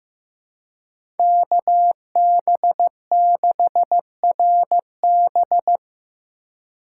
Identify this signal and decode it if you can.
Morse code, heard as a keyed tone, about 15 wpm, 700 Hz. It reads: KB6RB
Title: USA Call Signs - 5 Characters - 4x Character Spacing - 15wpm